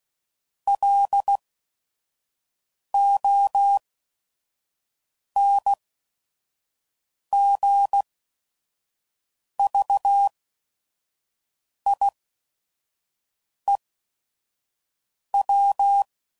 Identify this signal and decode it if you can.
Morse code, heard as a keyed tone, about 16 wpm, spaced as for 5 wpm, 800 Hz. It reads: LONGVIEW
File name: CW Code Groups @5WPM / fromSM's / Cities